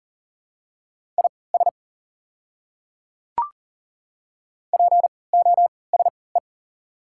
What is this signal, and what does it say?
Morse code, heard as a keyed tone, about 37 wpm, spaced as for 22 wpm, 690 Hz. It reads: IS POSE
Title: 3x Character Spacing - Top 500 Words - 40wpm